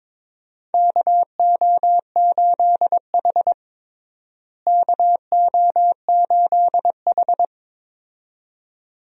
Morse code, heard as a keyed tone, about 22 wpm, 700 Hz. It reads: KO8H KO8H
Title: USA Call Signs - 4 Characters - Repeated 2x - 22wpm